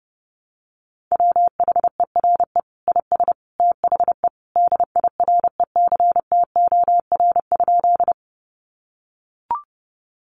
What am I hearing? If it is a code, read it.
Morse code, heard as a keyed tone, about 30 wpm, 700 Hz. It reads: WHERE IS THE DIRECTOR?
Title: Sentences from Top 1300 Words 30wpm